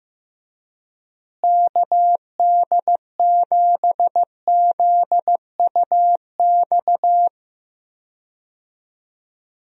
Morse code, heard as a keyed tone, about 16 wpm, 700 Hz. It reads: KD7ZUX
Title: USA Call Signs - 6 Characters - Repeated 2x - 15wpm